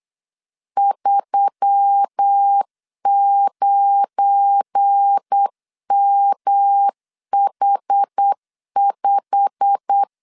Morse code, heard as a keyed tone, about 8 wpm, 800 Hz. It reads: 39MH5
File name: LEZIONI / 8 YJ45 / 8 YJ45 8wpm 3214